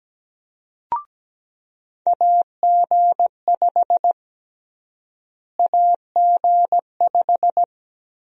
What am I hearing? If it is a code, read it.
Morse code, heard as a keyed tone, about 17 wpm, 700 Hz. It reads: AG5 AG5
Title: USA Call Sign Prefixes - Repeated 2x - 17wpm